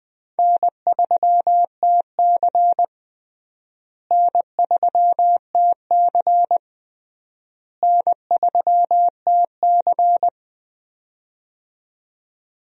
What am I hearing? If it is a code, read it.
Morse code, heard as a keyed tone, about 21 wpm, 700 Hz. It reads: N3TC N3TC N3TC
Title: USA Call Signs - 4 Characters - Repeated 3x - 20wpm